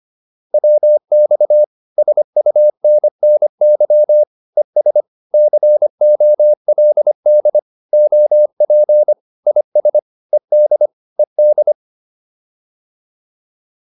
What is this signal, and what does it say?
Morse code, heard as a keyed tone, about 25 wpm, 600 Hz. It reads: WX SUNNY ES COLD OP IS ED ED